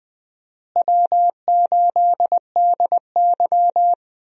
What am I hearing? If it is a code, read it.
Morse code, heard as a keyed tone, about 20 wpm, 700 Hz. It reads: W8DY
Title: USA Call Signs - 4 Characters - Repeated 2x - 20wpm